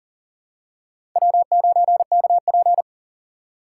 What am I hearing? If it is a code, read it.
Morse code, heard as a keyed tone, about 40 wpm, 690 Hz. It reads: W9KP
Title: USA Call Signs - 4 Characters - 2x Character Spacing - 40wpm